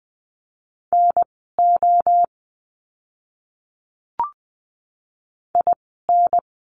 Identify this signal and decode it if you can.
Morse code, heard as a keyed tone, about 20 wpm, spaced as for 14 wpm, 700 Hz. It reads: NO IN
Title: 030 Lesson 6f - Letter i - Words 20wpm